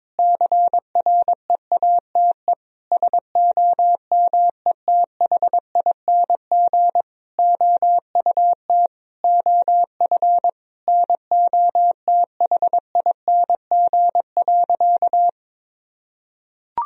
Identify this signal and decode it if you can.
Morse code, heard as a keyed tone, about 22 wpm, 700 Hz. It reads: CREATE SOMETHING OUT OF NOTHING.
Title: Sentences from Top 1000 Words - Review Part 3 - 22wpm